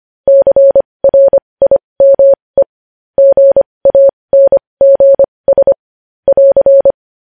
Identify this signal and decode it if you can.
Morse code, heard as a keyed tone, about 25 wpm, spaced as for 20 wpm, 550 Hz. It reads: CRIME GANGS <AR>
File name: podcast / News headlines-1x-25x20